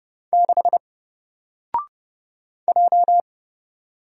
Morse code, heard as a keyed tone, about 30 wpm, 700 Hz. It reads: 6 J
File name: Single-Letter-Number-30wpm